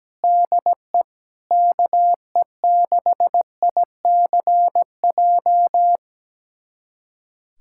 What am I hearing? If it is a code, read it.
Morse code, heard as a keyed tone, about 17 wpm, 700 Hz. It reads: DE KE6ICJ
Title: USA Callsign DE Callsign 17wpm